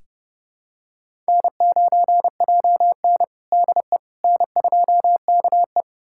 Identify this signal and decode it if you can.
Morse code, heard as a keyed tone, about 32 wpm, 700 Hz. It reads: N9JN DE N2KE
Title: USA Callsign DE Callsign - 4 Characters - 30wpm